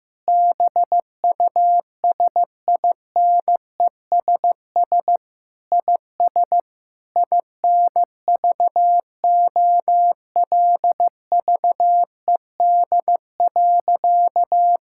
Morse code, heard as a keyed tone, about 15 wpm, 700 Hz. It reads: BUSINESS IS INVOLVED.
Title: Sentences from Top 1000 Words - Review Part 5 - 15wpm